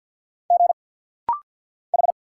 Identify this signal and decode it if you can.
Morse code, heard as a keyed tone, about 50 wpm, 700 Hz. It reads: G H
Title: Single Letters - Rapid-Fire 50wpm